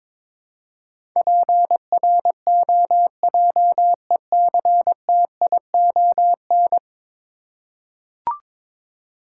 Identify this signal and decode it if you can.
Morse code, heard as a keyed tone, about 22 wpm, 700 Hz. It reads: PROJECTION
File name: practice / 22wpm / 10 Letter Words - First 3 Letters Given - 22wpm